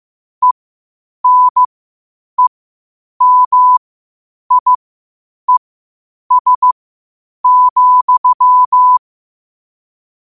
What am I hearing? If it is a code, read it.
Morse code, heard as a keyed tone, about 15 wpm, spaced as for 8 wpm, 1000 Hz. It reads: ENEMIES,
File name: qotdfiles / 8wpm